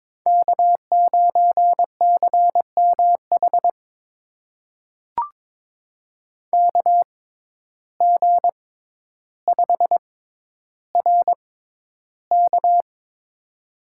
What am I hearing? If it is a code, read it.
Morse code, heard as a keyed tone, about 22 wpm, spaced as for 11 wpm, 700 Hz. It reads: K9CMH KG5RK
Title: USA Call Signs - 5 Characters - 6x Character Spacing - 22wpm